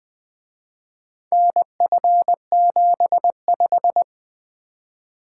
Morse code, heard as a keyed tone, about 24 wpm, 700 Hz. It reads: NF75
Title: Speed-Racing - Four Letter-Number Combinations - 20wpm